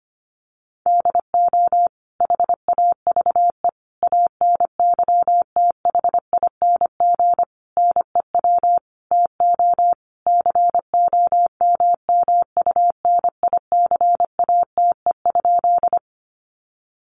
Morse code, heard as a keyed tone, about 25 wpm, 700 Hz. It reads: DO HAVE ANYTHING NEW TO COMMUNICATE?
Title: Sentences from Top 1300 Words 25wpm